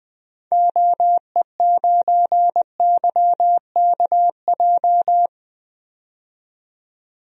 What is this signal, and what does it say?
Morse code, heard as a keyed tone, about 20 wpm, 700 Hz. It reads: OE9YKJ
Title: DXCC-100 Call Signs - 6 Characters - 20wpm